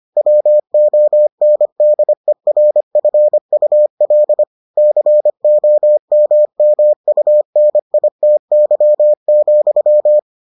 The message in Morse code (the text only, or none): WONDERFUL COMMUNITY,